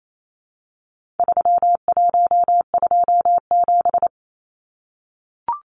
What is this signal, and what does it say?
Morse code, heard as a keyed tone, about 28 wpm, 700 Hz. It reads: 3127
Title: Numbers - 4-Digit 28wpm